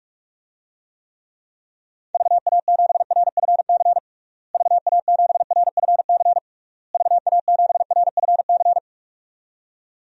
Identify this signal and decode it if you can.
Morse code, heard as a keyed tone, about 41 wpm, 700 Hz. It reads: VA7RFC VA7RFC VA7RFC
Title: DXCC-100 Call Signs - 6 Characters - Repeated 3x - 45wpm